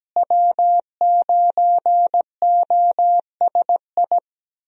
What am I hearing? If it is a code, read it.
Morse code, heard as a keyed tone, about 17 wpm, 700 Hz. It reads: W9OSI
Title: USA Call Signs - 5 Characters - Repeated 3x - 17wpm